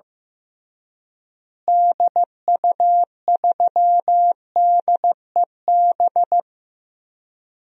Morse code, heard as a keyed tone, about 15 wpm, 700 Hz. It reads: DU3DEB
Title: DXCC-100 Call Signs - 6 Characters - Repeated 3x - 15wpm